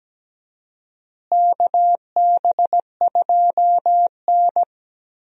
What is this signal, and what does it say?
Morse code, heard as a keyed tone, about 17 wpm, 700 Hz. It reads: KB2N
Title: USA Call Signs - 4 Characters - Repeated 3x - 17wpm